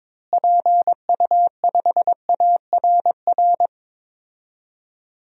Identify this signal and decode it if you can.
Morse code, heard as a keyed tone, about 22 wpm, 700 Hz. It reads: PU5ARR
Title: DXCC-100 Call Signs - 6 Characters - Repeated 2x - 22wpm